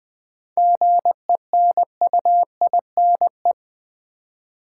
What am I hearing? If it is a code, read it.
Morse code, heard as a keyed tone, about 20 wpm, 700 Hz. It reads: GENUINE
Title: 7 Letter Words - First 3 Letters Given - 20wpm